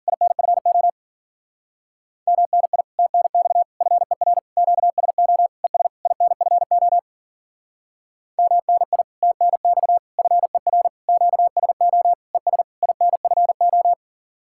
Morse code, heard as a keyed tone, about 52 wpm, 700 Hz. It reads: INFO MNI TNX FER QSO ES INFO MNI TNX FER QSO ES INFO